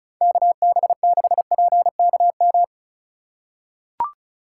Morse code, heard as a keyed tone, about 36 wpm, 700 Hz. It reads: KB6PKM
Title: USA Call Signs - 6 Characters - 2x Character Spacing - 35wpm